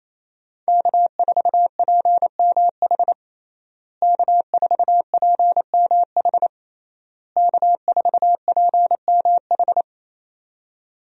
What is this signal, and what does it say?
Morse code, heard as a keyed tone, about 28 wpm, 700 Hz. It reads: K4PMH K4PMH K4PMH
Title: USA Call Signs - 5 Characters - Repeated 3x - 28wpm